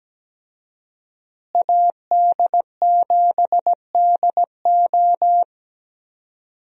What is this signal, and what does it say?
Morse code, heard as a keyed tone, about 17 wpm, 700 Hz. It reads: AD7DO
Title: USA Call Signs - 5 Characters - Repeated 3x - 17wpm